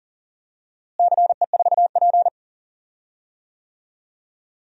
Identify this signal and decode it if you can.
Morse code, heard as a keyed tone, about 40 wpm, 700 Hz. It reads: CE4P
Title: DXCC-100 Call Signs - 4 Characters - Repeated 2x - 40wpm